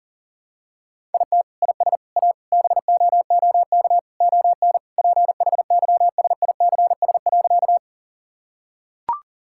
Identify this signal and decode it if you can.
Morse code, heard as a keyed tone, about 40 wpm, 700 Hz. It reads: IT IS A BOOK ON PHYSICS.
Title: Sentences from Top 1000 Words - Review Part 5 - 40wpm